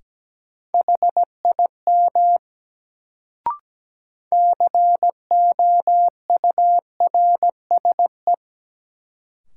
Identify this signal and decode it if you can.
Morse code, heard as a keyed tone, about 17 wpm, 700 Hz. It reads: HIM COURSE
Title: Sets of 2 Words - New Word Drill - 17wpm